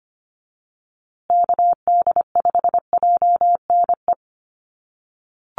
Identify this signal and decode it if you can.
Morse code, heard as a keyed tone, about 25 wpm, 700 Hz. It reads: KD5JNE
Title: USA Call Signs Repeated 3x 25wpm